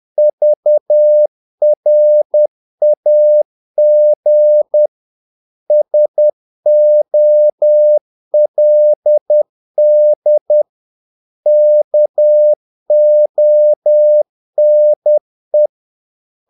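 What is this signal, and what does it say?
Morse code, heard as a keyed tone, about 10 wpm, 600 Hz. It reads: VRAG SOLD KONE
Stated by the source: Morse practice of short Danish words at 10 wpm, from CW med Gnister.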